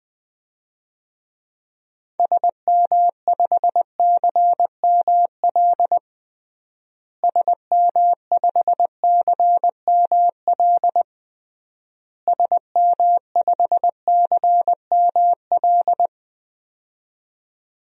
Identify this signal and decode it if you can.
Morse code, heard as a keyed tone, about 20 wpm, 700 Hz. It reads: SM5CML SM5CML SM5CML
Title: DXCC-100 Call Signs - 6 Characters - Repeated 3x - 20wpm